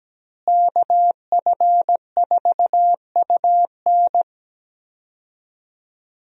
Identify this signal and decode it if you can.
Morse code, heard as a keyed tone, about 17 wpm, 700 Hz. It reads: KF4UN
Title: USA Call Signs - 5 Characters - Repeated 3x - 17wpm